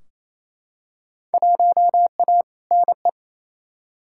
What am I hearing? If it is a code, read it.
Morse code, heard as a keyed tone, about 28 wpm, 700 Hz. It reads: 1A NE
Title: Field Day Exchanges Encore - 28wpm